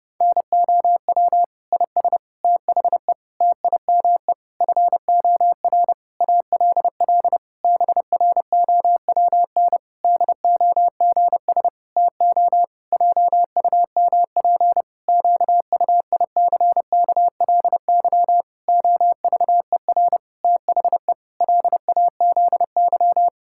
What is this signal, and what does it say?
Morse code, heard as a keyed tone, about 30 wpm, 700 Hz. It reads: NOW IS THE TIME FOR ALL BROWN DOGS TO JUMP QUICKLY OVER THE LAZY
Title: Pangram Challenge v2 - 30wpm